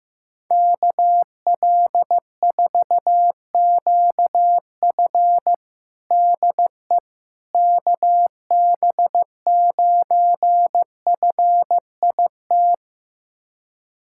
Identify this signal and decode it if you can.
Morse code, heard as a keyed tone, about 15 wpm, 700 Hz. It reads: KL4QF DE KB9FIT
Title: USA Callsign DE Callsign 15wpm